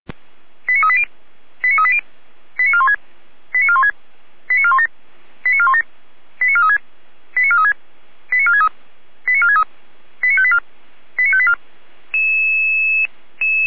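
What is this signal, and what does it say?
Five-tone selective call (ZVEI), heard as quick runs of five tones, tones of about 70 ms. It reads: 89380 89380 89416 89416 89417 89417 89436 89436 89463 89463 89584 89584
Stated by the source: Five-tone ZVEI selective call received on SDR.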